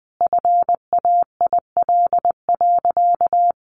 Morse code, heard as a keyed tone, about 20 wpm, 700 Hz. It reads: FAIL.
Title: Sentences from Top 1100 Words 20wpm